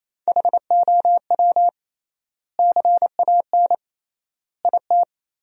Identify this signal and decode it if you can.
Morse code, heard as a keyed tone, about 28 wpm, 700 Hz. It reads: HOW CAN IT
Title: 3x Word-Spacing - Sentences from Top 100 Words - 28wpm